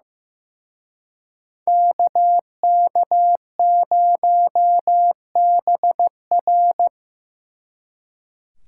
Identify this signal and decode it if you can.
Morse code, heard as a keyed tone, about 18 wpm, 700 Hz. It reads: KK0BR
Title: Speed Racing - USA 2x2 Call Signs 15wpm